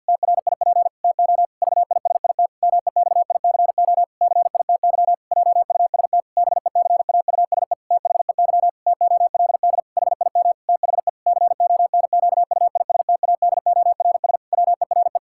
Morse code, heard as a keyed tone, about 50 wpm, 700 Hz. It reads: TRIP TO VISIT MEXICO CITY JUST BECAUSE THEY TOLD HIM THE CONQUISTADORS WERE
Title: Pangram Challenge v2 - 50wpm